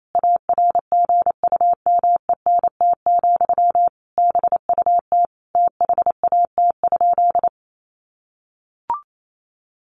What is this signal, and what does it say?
Morse code, heard as a keyed tone, about 28 wpm, 700 Hz. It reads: ARGUMENT, BUT THAT?
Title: Sentences from Top 1400 Words 28wpm